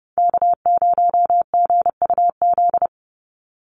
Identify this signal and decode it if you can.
Morse code, heard as a keyed tone, about 30 wpm, 700 Hz. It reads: K0GUZ
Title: Call Signs Repeated 3x - 30wpm